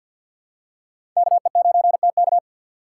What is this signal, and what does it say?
Morse code, heard as a keyed tone, about 49 wpm, 690 Hz. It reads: KE9TK
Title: USA Call Signs - 2x Character Spacing - 50wpm